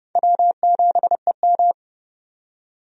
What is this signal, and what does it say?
Morse code, heard as a keyed tone, about 31 wpm, 700 Hz. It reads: W7EM
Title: USA Call Signs - 4 Characters - 30wpm